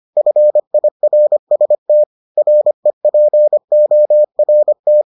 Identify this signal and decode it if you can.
Morse code, heard as a keyed tone, about 25 wpm, 600 Hz. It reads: FIRST REPORT